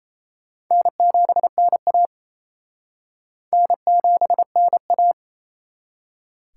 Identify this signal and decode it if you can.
Morse code, heard as a keyed tone, about 36 wpm, 700 Hz. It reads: N7NA N7NA